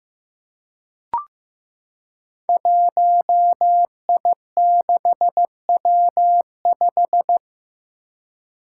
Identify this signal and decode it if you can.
Morse code, heard as a keyed tone, about 15 wpm, 700 Hz. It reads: 1I6W5
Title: Five Challenging Letter-Number Combinations 15wpm